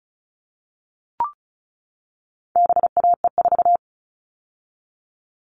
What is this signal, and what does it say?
Morse code, heard as a keyed tone, about 34 wpm, 700 Hz. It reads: BAE4
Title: All Cut Numbers - 4-Digit 35wpm